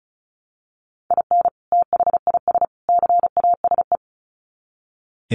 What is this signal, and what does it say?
Morse code, heard as a keyed tone, about 34 wpm, 700 Hz. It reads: IN THIS CASE
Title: Sets of 3 Words 35wpm